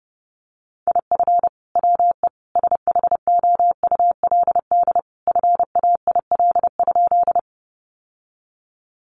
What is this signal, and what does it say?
Morse code, heard as a keyed tone, about 30 wpm, 700 Hz. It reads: IF WE SHOULD FAIL?
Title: Sentences from Top 1100 Words 30wpm